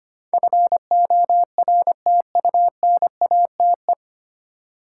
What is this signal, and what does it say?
Morse code, heard as a keyed tone, about 25 wpm, 700 Hz. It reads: FORTUNATE
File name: Nine Letter Words - 25wpm